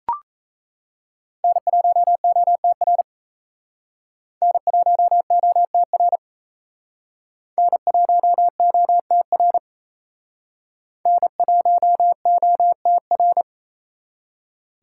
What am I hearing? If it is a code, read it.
Morse code, heard as a keyed tone, about 35 wpm, 700 Hz. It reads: N1OTR N1OTR N1OTR N1OTR